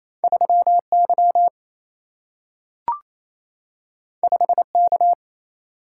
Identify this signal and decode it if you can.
Morse code, heard as a keyed tone, about 28 wpm, 700 Hz. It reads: 3Y 5K
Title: Single-Letter Single-Number 28wpm